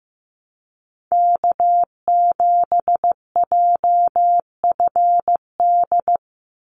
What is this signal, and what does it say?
Morse code, heard as a keyed tone, about 15 wpm, 700 Hz. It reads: K7JFD
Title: NA SOTA Activator Callsigns 15wpm